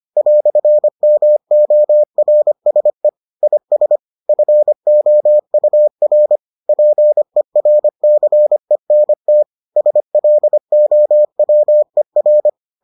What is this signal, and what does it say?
Morse code, heard as a keyed tone, about 25 wpm, 600 Hz. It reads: "MORSE IS FOUR PERCENT SLOWER